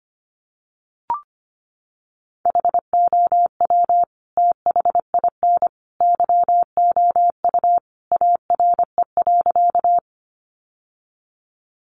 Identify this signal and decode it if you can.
Morse code, heard as a keyed tone, about 25 wpm, 700 Hz. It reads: HOW THIN YOU ARE.
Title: Sentences from Top 1200 Words 25wpm